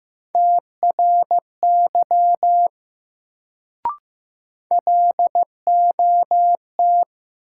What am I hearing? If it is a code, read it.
Morse code, heard as a keyed tone, about 15 wpm, 700 Hz. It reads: TRY LOT
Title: Sets of 2 Words - New Word Drill - 15wpm